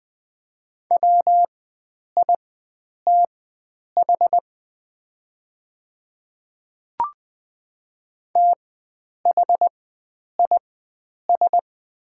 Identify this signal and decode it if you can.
Morse code, heard as a keyed tone, about 20 wpm, spaced as for 9 wpm, 690 Hz. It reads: WITH THIS
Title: 4x Character Spacing - Top 100 Words - 20wpm